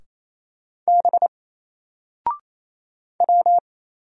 Morse code, heard as a keyed tone, about 27 wpm, 700 Hz. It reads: B W
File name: Single-Letter-Number-28wpm